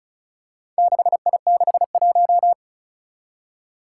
Morse code, heard as a keyed tone, about 36 wpm, 700 Hz. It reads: 6I61
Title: Four Challenging Letter-Number Combinations 35wpm